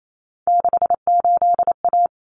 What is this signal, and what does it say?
Morse code, heard as a keyed tone, about 28 wpm, 700 Hz. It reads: 68A
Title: Common Cut Numbers - 3-Digit 28wpm